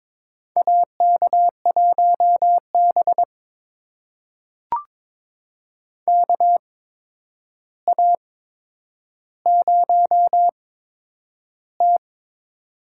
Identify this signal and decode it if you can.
Morse code, heard as a keyed tone, about 23 wpm, spaced as for 10 wpm, 700 Hz. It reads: AK1B KA0T
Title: USA Call Signs - 4 Characters - 8x Character Spacing - 22wpm